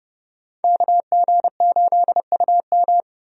Morse code, heard as a keyed tone, about 32 wpm, 700 Hz. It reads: KG8UM
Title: USA Call Signs - 5 Characters - Repeated 2x - 30wpm